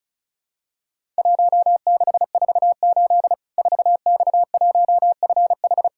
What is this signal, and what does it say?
Morse code, heard as a keyed tone, about 36 wpm, 700 Hz. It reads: 1648 4X1FH
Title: Worked All Europe DX Contest - QTC Exchange - 35wpm